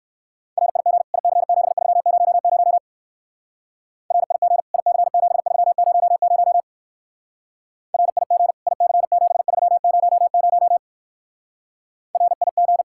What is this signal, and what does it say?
Morse code, heard as a keyed tone, about 54 wpm, 700 Hz. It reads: RIG IC7300 RIG IC7300 RIG IC7300 RIG
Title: Speed-Racing Common QSO Phrases - 45wpm